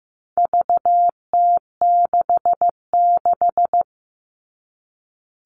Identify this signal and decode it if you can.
Morse code, heard as a keyed tone, about 15 wpm, 700 Hz. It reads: VT66
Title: All Cut Numbers - 4-Digit 15wpm